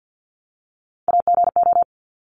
Morse code, heard as a keyed tone, about 50 wpm, 700 Hz. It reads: AGO
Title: Top 500 Words 50wpm